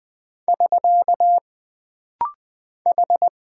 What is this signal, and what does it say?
Morse code, heard as a keyed tone, about 20 wpm, 700 Hz. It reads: <SK> H
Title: Single Character (Letter:Number:Punctuation:Prosign) - Rapid-Fire - 20wpm